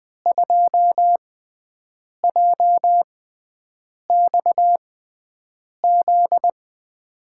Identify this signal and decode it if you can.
Morse code, heard as a keyed tone, about 21 wpm, spaced as for 7 wpm, 700 Hz. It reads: 2JXZ
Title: USA Call Signs - 6x Character Spacing - 20wpm